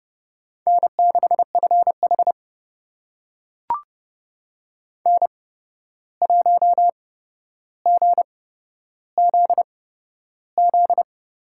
Morse code, heard as a keyed tone, about 29 wpm, spaced as for 12 wpm, 700 Hz. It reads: N6FH N1GZZ
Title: USA Call Signs - 8x Character Spacing - 30wpm